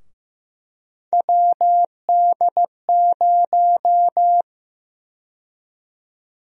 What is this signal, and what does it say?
Morse code, heard as a keyed tone, about 15 wpm, 700 Hz. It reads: WD0
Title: USA Call Sign Prefixes - Repeated 2x - 15wpm